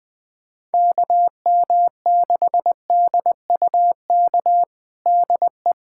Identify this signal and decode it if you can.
Morse code, heard as a keyed tone, about 21 wpm, 700 Hz. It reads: KM6DUK DE
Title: USA Callsign DE Callsign - 6 Characters - 20wpm